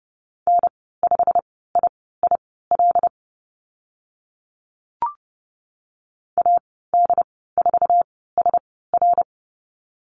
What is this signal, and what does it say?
Morse code, heard as a keyed tone, about 30 wpm, spaced as for 17 wpm, 700 Hz. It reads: N5IIL AD4SR
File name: Lesson 15g - Number 5 - Call Signs 30wpm